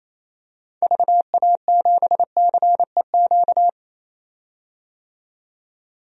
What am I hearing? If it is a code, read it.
Morse code, heard as a keyed tone, about 28 wpm, 700 Hz. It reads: VA7CEQ
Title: DXCC-100 Call Signs - 6 Characters - 28wpm